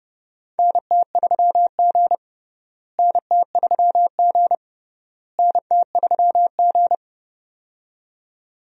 Morse code, heard as a keyed tone, about 30 wpm, 700 Hz. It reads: NT3G NT3G NT3G
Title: USA Call Signs - 4 Characters - Repeated 3x - 30wpm